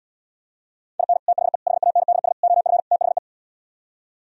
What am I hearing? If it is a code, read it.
Morse code, heard as a keyed tone, about 77 wpm, 700 Hz. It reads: IN THE HANDS OF THE